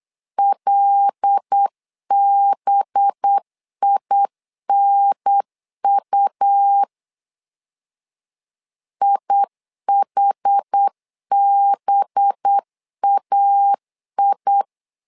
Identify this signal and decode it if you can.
Morse code, heard as a keyed tone, about 8 wpm, 800 Hz. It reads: LBINU IHBAI
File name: LEZIONI / 5 LBVC / 5 LB 8wpm 2816